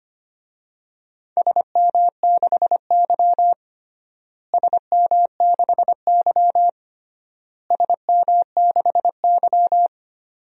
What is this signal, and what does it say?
Morse code, heard as a keyed tone, about 25 wpm, 700 Hz. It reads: SM6Y SM6Y SM6Y
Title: DXCC-100 Call Signs - 4 Characters - Repeated 3x - 25wpm